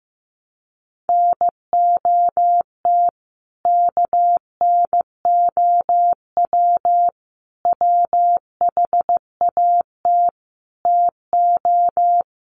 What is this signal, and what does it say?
Morse code, heard as a keyed tone, about 15 wpm, 700 Hz. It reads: NOT KNOW WHAT TO
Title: Sets of 5 Words 15wpm